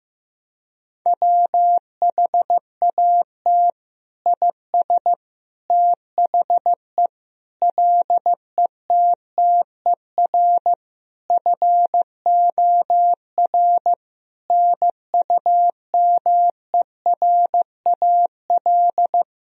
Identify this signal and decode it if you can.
Morse code, heard as a keyed tone, about 15 wpm, 700 Hz. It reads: WHAT IS THE LETTER FOR NUMERAL
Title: Sentences from Top 1000 Words - Review Part 2 - 15wpm